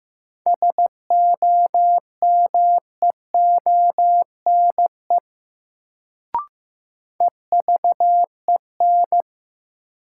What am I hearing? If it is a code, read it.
Morse code, heard as a keyed tone, about 15 wpm, 700 Hz. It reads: SOMEONE EVEN
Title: Sentences from Top 600 Words - New Word Drill - 15wpm